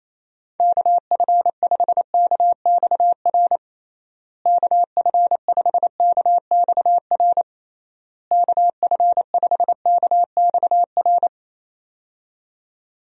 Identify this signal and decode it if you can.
Morse code, heard as a keyed tone, about 28 wpm, 700 Hz. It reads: KF5KXR KF5KXR KF5KXR
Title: USA Call Signs - 6 Characters - Repeated 3x - 28wpm